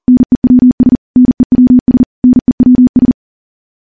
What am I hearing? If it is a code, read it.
Morse code, heard as a keyed tone, about 40 wpm, 250 Hz. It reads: NEWS NEWS NEWS
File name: morse / morse-training-40WPM